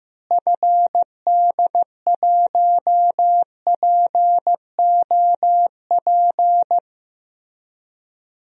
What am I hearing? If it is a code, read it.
Morse code, heard as a keyed tone, about 15 wpm, 700 Hz. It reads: FD1POP
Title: DXCC-100 Call Signs - 6 Characters - Repeated 2x - 15wpm